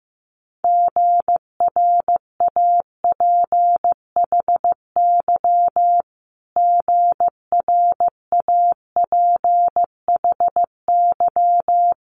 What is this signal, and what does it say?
Morse code, heard as a keyed tone, about 16 wpm, 700 Hz. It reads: GRAPHY GRAPHY
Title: Multi-letter Phonemes Repeated 3x 15wpm